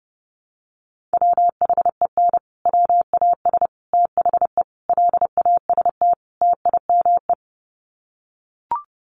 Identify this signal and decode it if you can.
Morse code, heard as a keyed tone, about 28 wpm, 700 Hz. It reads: WHEN WAS THE LAST TIME
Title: Sets of 5 Words Encore 30wpm